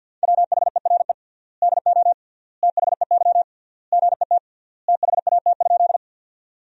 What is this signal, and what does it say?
Morse code, heard as a keyed tone, about 48 wpm, 700 Hz. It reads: WHERE DO THEY GET THAT?
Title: 3x Word-Spacing - Sentences from Top 200 Words - 50wpm